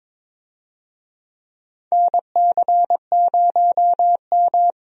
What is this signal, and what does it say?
Morse code, heard as a keyed tone, about 22 wpm, 700 Hz. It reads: NC0M
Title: USA Call Signs - 4 Characters - 22wpm